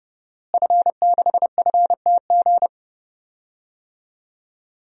Morse code, heard as a keyed tone, about 30 wpm, 700 Hz. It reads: F6FTG
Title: DXCC-100 Call Signs - 5 Characters - Repeated 2x - 30wpm